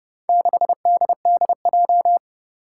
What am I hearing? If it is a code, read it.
Morse code, heard as a keyed tone, about 30 wpm, 700 Hz. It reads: 6DDJ
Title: Four Challenging Letter-Number Combinations 30wpm